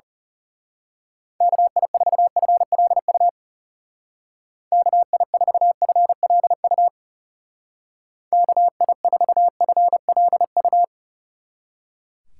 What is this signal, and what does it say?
Morse code, heard as a keyed tone, about 34 wpm, 700 Hz. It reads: KI4FLU KI4FLU KI4FLU
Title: Speed Racing - USA Call Signs 30wpm